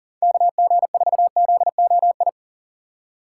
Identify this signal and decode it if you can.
Morse code, heard as a keyed tone, about 37 wpm, 700 Hz. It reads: KG4ZOI
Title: USA Call Signs - 6 Characters - 4x Character Spacing - 40wpm